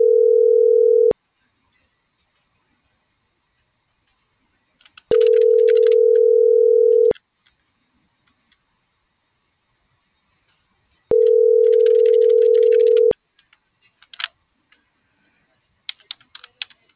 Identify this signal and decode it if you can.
Ringback tone (North America), 440 + 480 Hz.